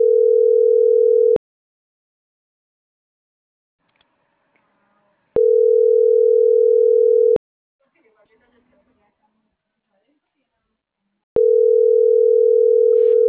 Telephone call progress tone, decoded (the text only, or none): ringback tone